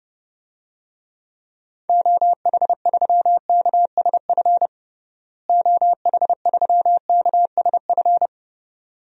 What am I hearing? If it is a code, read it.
Morse code, heard as a keyed tone, about 30 wpm, 700 Hz. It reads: OH3KSF OH3KSF
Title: DXCC-100 Call Signs - 6 Characters - Repeated 2x - 30wpm